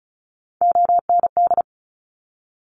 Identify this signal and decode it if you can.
Morse code, heard as a keyed tone, about 36 wpm, 700 Hz. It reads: OND
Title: Top Three Letter Combinations in Words 35wpm